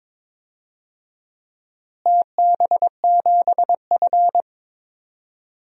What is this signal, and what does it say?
Morse code, heard as a keyed tone, about 22 wpm, 700 Hz. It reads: TB7F
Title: DXCC-100 Call Signs - 4 Characters - Repeated 2x - 22wpm